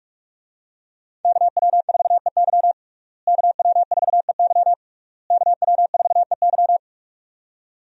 Morse code, heard as a keyed tone, about 46 wpm, 700 Hz. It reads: KW4EY KW4EY KW4EY
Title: USA Call Signs - 5 Characters - Repeated 3x - 45wpm